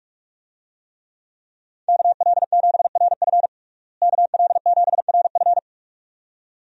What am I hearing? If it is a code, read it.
Morse code, heard as a keyed tone, about 47 wpm, 700 Hz. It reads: KL7RF KL7RF
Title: DXCC-100 Call Signs - 5 Characters - Repeated 2x - 45wpm